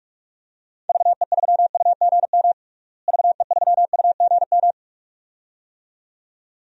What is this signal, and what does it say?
Morse code, heard as a keyed tone, about 46 wpm, 700 Hz. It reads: VE3UGM VE3UGM
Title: DXCC-100 Call Signs - 6 Characters - Repeated 2x - 45wpm